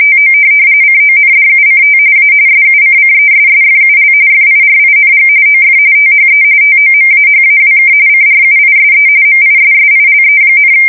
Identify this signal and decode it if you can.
RTTY, about 45.45 baud. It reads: TO CONDUCT A SUCCESSFUL FIRING OF THE TOMAHAWK CRUISE MISSILE,
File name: fsk170 / USS-New-Jersey